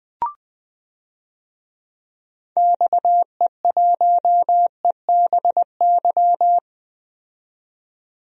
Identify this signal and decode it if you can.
Morse code, heard as a keyed tone, about 20 wpm, 700 Hz. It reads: XE1EBY
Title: DXCC-100 Call Signs - 6 Characters - 20wpm